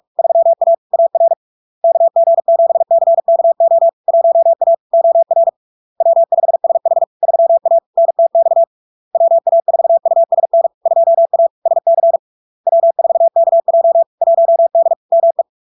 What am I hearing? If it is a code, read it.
Morse code, heard as a keyed tone, about 46 wpm, 670 Hz. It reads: 3A AR KG7CKO 1A OR W5SH 3A NTX WA4USN 2A SC W4KJ 1D ME
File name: LARK-FD-2022-Log-in-CW-45wpm-670-Hz